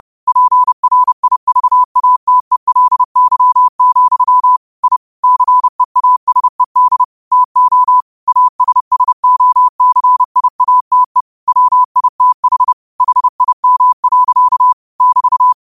Morse code, heard as a keyed tone, about 30 wpm, 1000 Hz. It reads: PRIVATELY, I CEASED TO ASSOCIATE WITH HIM. <BT>